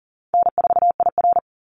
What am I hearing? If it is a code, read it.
Morse code, heard as a keyed tone, about 40 wpm, 700 Hz. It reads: N4IR
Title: Call Signs Repeated 3x - 40wpm